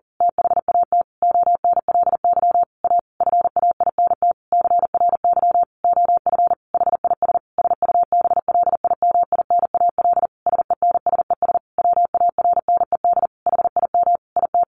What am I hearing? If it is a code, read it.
Morse code, heard as a keyed tone, about 40 wpm, 700 Hz. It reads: THAT ONLY A FAINT CRY OF HIS SUBLIMINAL SENSES WARNED HIM IT
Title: A Sound Of Thunder 40wpm